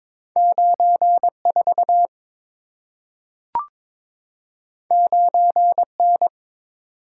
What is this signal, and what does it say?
Morse code, heard as a keyed tone, about 22 wpm, 700 Hz. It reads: 94 9N
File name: practice / 22wpm / Two Letter-Number Combinations - 22wpm